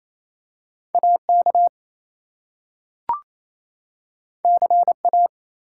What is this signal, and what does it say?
Morse code, heard as a keyed tone, about 28 wpm, 700 Hz. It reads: AK CA
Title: US States and Canadian Provinces 28wpm